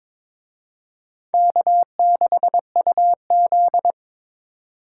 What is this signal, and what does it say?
Morse code, heard as a keyed tone, about 27 wpm, 700 Hz. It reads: K6UZ
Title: Speed Racing - USA 1x2 Call Signs 22wpm